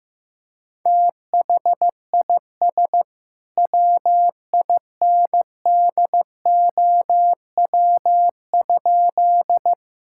Morse code, heard as a keyed tone, about 15 wpm, 700 Hz. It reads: THIS WINDOW?
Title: Sentences from Top 1600 Words 15wpm